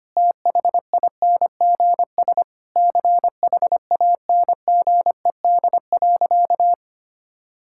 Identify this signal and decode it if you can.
Morse code, heard as a keyed tone, about 25 wpm, 700 Hz. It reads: THINGS CHANGED.
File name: practice / 25wpm / Sentences with 4-Word Phrases - 25wpm